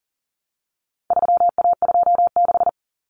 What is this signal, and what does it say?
Morse code, heard as a keyed tone, about 40 wpm, 700 Hz. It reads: 3A26
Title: Common Cut Numbers - 4-Digit 40wpm